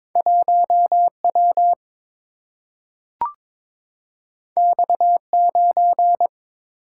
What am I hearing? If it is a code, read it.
Morse code, heard as a keyed tone, about 22 wpm, 700 Hz. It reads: 1 W X 9
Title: Single-Letter Single-Number 22wpm